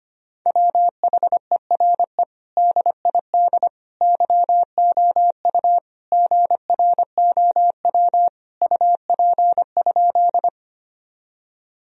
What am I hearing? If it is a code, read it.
Morse code, heard as a keyed tone, about 25 wpm, 700 Hz. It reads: WHERE DID YOU GROW UP?
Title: Sentences from Top 1000 Words - Review Part 1 - 25wpm